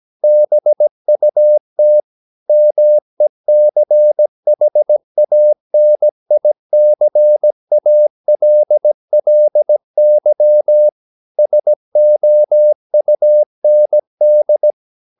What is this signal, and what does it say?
Morse code, heard as a keyed tone, about 17 wpm, 600 Hz. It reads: BUT MECHANICALLY SOUND
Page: Romanen Never af Ken Follet på engelsk.